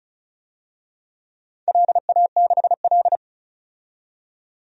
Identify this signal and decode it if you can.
Morse code, heard as a keyed tone, about 34 wpm, 700 Hz. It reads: LA6L
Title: DXCC-100 Call Signs - 4 Characters - 35wpm